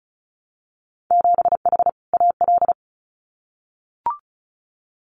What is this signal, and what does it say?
Morse code, heard as a keyed tone, about 36 wpm, 700 Hz. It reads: 7H AL
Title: Winter Field Day Exchanges Encore 35wpm